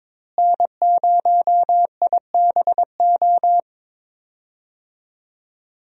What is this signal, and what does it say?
Morse code, heard as a keyed tone, about 22 wpm, 700 Hz. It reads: N0IBO
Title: USA Call Signs - 5 Characters - Repeated 3x - 22wpm